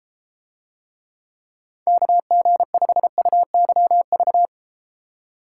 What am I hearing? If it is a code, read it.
Morse code, heard as a keyed tone, about 32 wpm, 700 Hz. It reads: KG5UYV